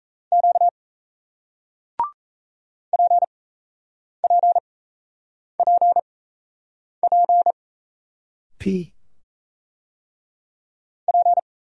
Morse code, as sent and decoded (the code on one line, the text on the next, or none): --.- / .--. .--. .--. .--. / .--.
Q PPPP P